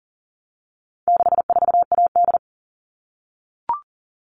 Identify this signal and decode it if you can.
Morse code, heard as a keyed tone, about 40 wpm, 700 Hz. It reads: 64AD
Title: All Cut Numbers - 4-Digit 40wpm